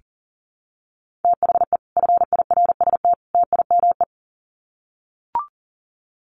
Morse code, heard as a keyed tone, about 40 wpm, 700 Hz. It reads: THE FIRST TIME
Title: Sets of 3 Words 40wpm